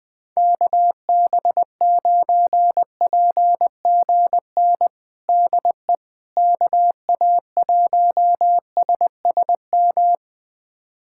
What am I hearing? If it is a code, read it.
Morse code, heard as a keyed tone, about 21 wpm, 700 Hz. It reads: KB9PGN DE KA1SSM
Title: USA Callsign DE Callsign - 6 Characters - 20wpm